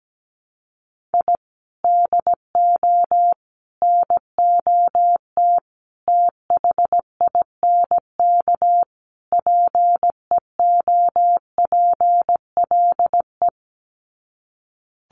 Morse code, heard as a keyed tone, about 17 wpm, 700 Hz. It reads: I DO NOT THINK PEOPLE
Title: Sets of 5 Words Encore2 17wpm